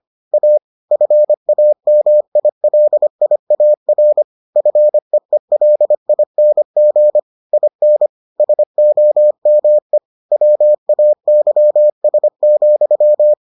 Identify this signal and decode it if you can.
Morse code, heard as a keyed tone, about 25 wpm, 600 Hz. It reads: A FAMILIAR FEELING IN SOME WAYS,